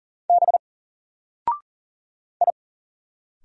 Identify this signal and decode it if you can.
Morse code, heard as a keyed tone, about 39 wpm, 700 Hz. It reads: B I
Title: Single Character (Letter:Number:Punctuation:Prosign) 40wpm